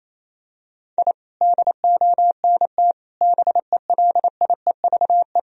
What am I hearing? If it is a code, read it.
Morse code, heard as a keyed tone, about 28 wpm, 700 Hz. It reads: I DONT BELIEVE
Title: Sentences from Top 1000 Words - Review Part 1 - 28wpm